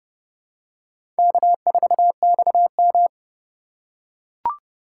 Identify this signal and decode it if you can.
Morse code, heard as a keyed tone, about 30 wpm, 700 Hz. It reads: K4XM
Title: USA Call Signs - 4 Characters - 4x Character Spacing - 30wpm